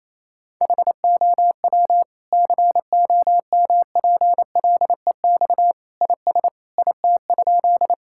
Morse code, heard as a keyed tone, about 28 wpm, 700 Hz. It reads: HOW COMPLEX IS IT?
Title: Sentences from Top 1000 Words - Review Part 6 - 28wpm